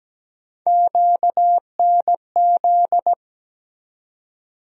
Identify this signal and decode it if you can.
Morse code, heard as a keyed tone, about 17 wpm, 700 Hz. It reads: QNZ
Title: USA Call Sign Postfixes - 3 Letters 17wpm